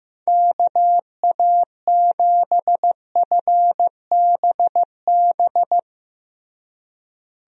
Morse code, heard as a keyed tone, about 16 wpm, 700 Hz. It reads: KA7FBB
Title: USA Call Signs - 6 Characters - Repeated 2x - 15wpm